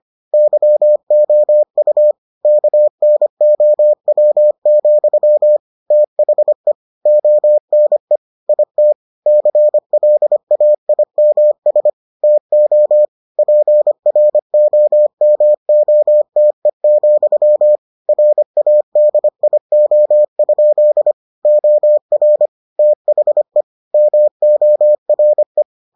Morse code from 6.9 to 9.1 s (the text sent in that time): ONE IT